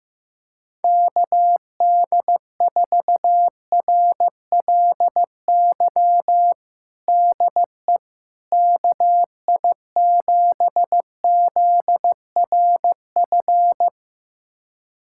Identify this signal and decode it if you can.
Morse code, heard as a keyed tone, about 16 wpm, 700 Hz. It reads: KD4RLY DE KI7ZRF
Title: USA Callsign DE Callsign 15wpm